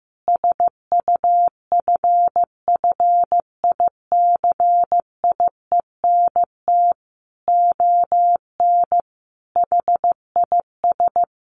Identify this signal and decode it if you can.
Morse code, heard as a keyed tone, about 15 wpm, 700 Hz. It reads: SUFFICIENT ON HIS
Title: Sentences from Top 1400 Words 15wpm